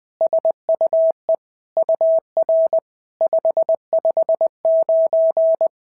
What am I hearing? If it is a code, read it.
Morse code, heard as a keyed tone, about 20 wpm, 650 Hz. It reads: SUE UR 559
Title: SOTA QSO Practice 20wpm